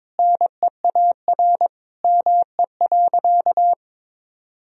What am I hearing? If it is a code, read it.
Morse code, heard as a keyed tone, about 22 wpm, 700 Hz. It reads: NEAR ME.
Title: Sentences from Top 1000 Words - Review Part 1 - 22wpm